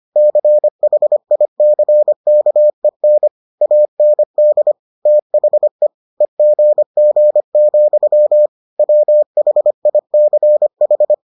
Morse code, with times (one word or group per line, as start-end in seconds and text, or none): 0.2-3.3 CHICKEN
3.6-4.7 AND
5.1-5.9 THE
6.2-8.5 EGG,
8.8-11.1 WHICH